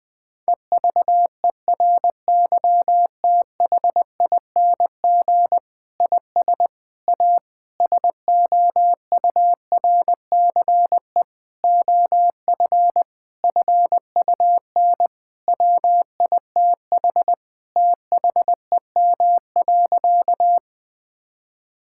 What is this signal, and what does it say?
Morse code, heard as a keyed tone, about 21 wpm, 700 Hz. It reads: EVERYTHING IS A SOURCE OF FUN WITH THEM.
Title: Sentences from Top 1000 Words - Review Part 5 - 20wpm